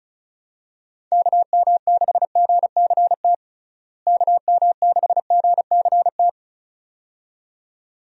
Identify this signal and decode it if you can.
Morse code, heard as a keyed tone, about 36 wpm, 700 Hz. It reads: KM6GCT KM6GCT
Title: USA Call Signs - 6 Characters - Repeated 2x - 35wpm